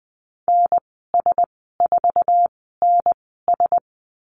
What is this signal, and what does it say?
Morse code, heard as a keyed tone, about 20 wpm, spaced as for 15 wpm, 700 Hz. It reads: NS4NS
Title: 140 Lesson 23g - Letter f - Call Signs 20wpm